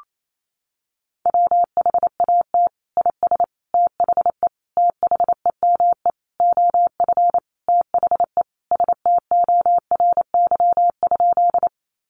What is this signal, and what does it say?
Morse code, heard as a keyed tone, about 28 wpm, 700 Hz. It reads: WHAT IS THE THEME OF THE STORY?
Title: Sentences from Top 1100 Words 28wpm